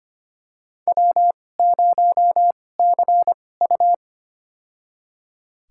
Morse code, heard as a keyed tone, about 25 wpm, spaced as for 18 wpm, 700 Hz. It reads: W0CU
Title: USA Call Signs - 4 Characters - 2x Character Spacing - 25wpm